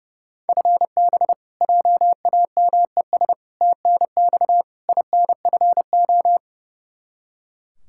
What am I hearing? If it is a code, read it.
Morse code, heard as a keyed tone, about 31 wpm, 700 Hz. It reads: FB JAMES TNX INFO
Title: Practice QSO - Exchange 3 - More Info - 30wpm